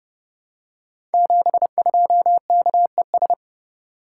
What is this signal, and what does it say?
Morse code, heard as a keyed tone, about 30 wpm, 700 Hz. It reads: 72KES
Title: Code Groups - 5 Letter:Numbers - 30wpm